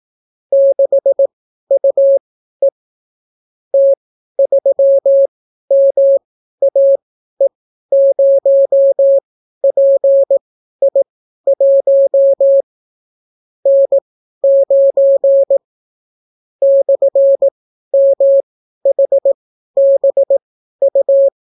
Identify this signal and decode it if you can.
Morse code, heard as a keyed tone, about 18 wpm, spaced as for 12 wpm, 550 Hz. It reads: 6UE T3MAE0PI1 N9 /MHBU